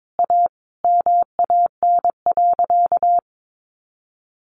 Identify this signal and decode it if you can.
Morse code, heard as a keyed tone, about 22 wpm, 700 Hz. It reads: A MAN.
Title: Sentences from Top 1400 Words 22wpm